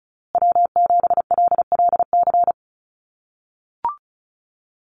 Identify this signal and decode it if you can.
Morse code, heard as a keyed tone, about 34 wpm, 700 Hz. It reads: W7LLC
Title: USA Call Signs Repeated 2x 35wpm